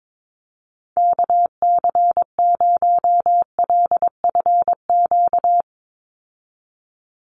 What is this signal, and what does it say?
Morse code, heard as a keyed tone, about 22 wpm, 700 Hz. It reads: KC0LFQ
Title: USA Call Signs 22wpm